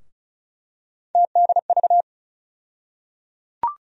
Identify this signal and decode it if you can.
Morse code, heard as a keyed tone, about 34 wpm, 700 Hz. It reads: TDV
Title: USA Call Sign Postfixes - 3 Letters 35wpm